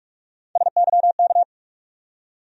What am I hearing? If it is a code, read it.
Morse code, heard as a keyed tone, about 41 wpm, 700 Hz. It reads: SYK
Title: USA Call Sign Postfixes - 3 Letters 45wpm